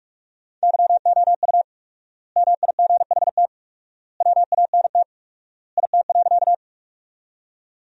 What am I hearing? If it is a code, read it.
Morse code, heard as a keyed tone, about 42 wpm, 700 Hz. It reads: YOU MIGHT WANT IT.
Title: 4x Word-Spacing - Sentences from Top 300 Words - 45wpm